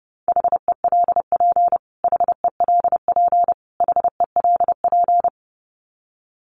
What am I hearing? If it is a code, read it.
Morse code, heard as a keyed tone, about 30 wpm, 700 Hz. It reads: HELP HELP HELP
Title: Top 200 Words Repeated 30wpm